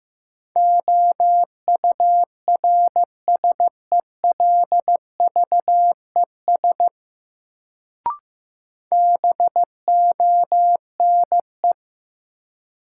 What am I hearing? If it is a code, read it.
Morse code, heard as a keyed tone, about 15 wpm, 700 Hz. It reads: OURSELVES BONE
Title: Sentences from Top 1600 Words - New Word Drill Alt - 15wpm